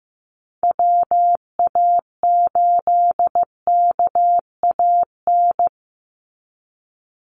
Morse code, heard as a keyed tone, about 15 wpm, 700 Hz. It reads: WA8KAN
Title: Call Signs Repeated 3x - 15wpm